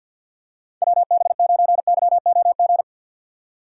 Morse code, heard as a keyed tone, about 53 wpm, 700 Hz. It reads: WB9YOG
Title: USA Call Signs - 6 Characters - 6x Character Spacing - 50wpm